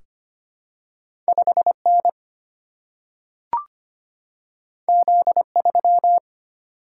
Morse code, heard as a keyed tone, about 25 wpm, 700 Hz. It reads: 5N Z3
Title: Single-Letter Single-Number 25wpm